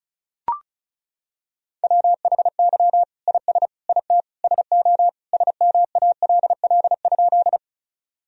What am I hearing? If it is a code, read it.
Morse code, heard as a keyed tone, about 36 wpm, 700 Hz. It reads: WHY IS IT SO SMALL?
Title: Sentences from Top 1000 Words - Review Part 1 - 35wpm